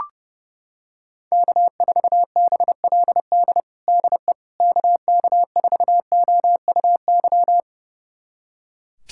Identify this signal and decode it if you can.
Morse code, heard as a keyed tone, about 32 wpm, 700 Hz. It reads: K4BLD DE KK4OUY
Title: USA Callsign DE Callsign 30wpm